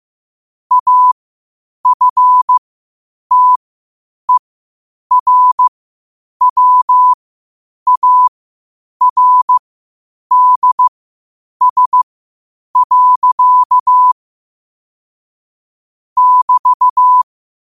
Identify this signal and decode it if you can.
Morse code, heard as a keyed tone, about 15 wpm, spaced as for 8 wpm, 1000 Hz. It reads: AFTERWARDS. <BT>
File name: qotdfiles / 8wpm